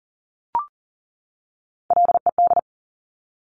Morse code, heard as a keyed tone, about 42 wpm, 700 Hz. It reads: LED
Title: Top Three Letter Combinations in Words 40wpm